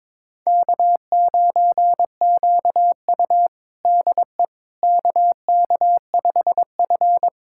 Morse code, heard as a keyed tone, about 22 wpm, 700 Hz. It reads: K9QU DE KK5F
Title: USA Callsign DE Callsign - 4 Characters - 22wpm